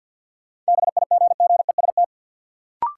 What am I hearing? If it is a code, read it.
Morse code, heard as a keyed tone, about 53 wpm, 700 Hz. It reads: BIGGEST
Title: Sets of 4 Words - New Word Drill - 50wpm